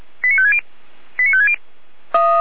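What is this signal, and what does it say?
Five-tone selective call (ZVEI), heard as quick runs of five tones, tones of about 70 ms. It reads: 89560 89560
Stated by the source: Received on an SDR as a ZVEI five-tone call.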